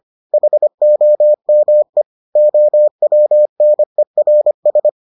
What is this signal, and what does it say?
Morse code, heard as a keyed tone, about 25 wpm, 600 Hz. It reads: HOME OWNERS